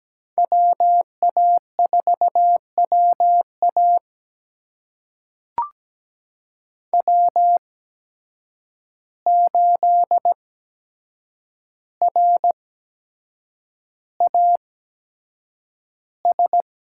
Morse code, heard as a keyed tone, about 17 wpm, spaced as for 7 wpm, 700 Hz. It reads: WA4WA W8RAS
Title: USA Call Signs - 5 Characters - 8x Character Spacing - 17wpm